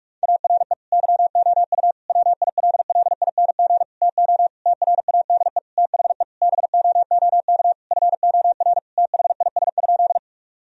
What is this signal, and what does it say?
Morse code, heard as a keyed tone, about 46 wpm, 700 Hz. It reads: ARE YOU WILLING TO TRADE THE BOOK FOR THIS?
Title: Sentences from Top 1000 Words - Review Part 4 - 45wpm